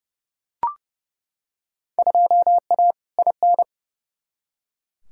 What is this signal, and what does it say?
Morse code, heard as a keyed tone, about 32 wpm, 700 Hz. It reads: 2A IN
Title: Field Day Exchanges - 30wpm